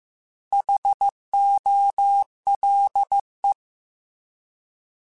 Morse code, heard as a keyed tone, about 15 wpm, 800 Hz. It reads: HOLE
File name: Alphabetical / Alphabetical mp3s / Hs @13